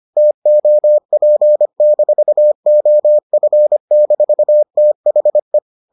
Morse code, heard as a keyed tone, about 25 wpm, 600 Hz. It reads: TOP-OF-THE